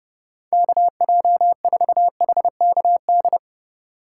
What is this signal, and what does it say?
Morse code, heard as a keyed tone, about 29 wpm, 700 Hz. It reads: KJ4HKD